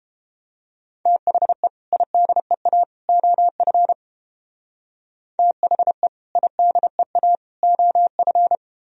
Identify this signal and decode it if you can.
Morse code, heard as a keyed tone, about 32 wpm, 700 Hz. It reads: THE IDEA OF THE IDEA OF